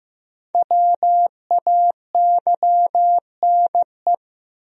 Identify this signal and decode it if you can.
Morse code, heard as a keyed tone, about 15 wpm, 700 Hz. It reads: WAYNE
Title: USA Common Names - Max 5 Letters - 15wpm